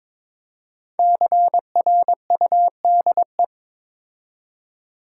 Morse code, heard as a keyed tone, about 23 wpm, 700 Hz. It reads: CRUDE
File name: 5 Letter Words - First 3 Letters Given - 22wpm